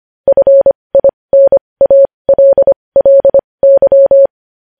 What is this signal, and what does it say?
Morse code, heard as a keyed tone, about 25 wpm, spaced as for 20 wpm, 550 Hz. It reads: FINALLY
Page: News headlines in morse code at 20 wpm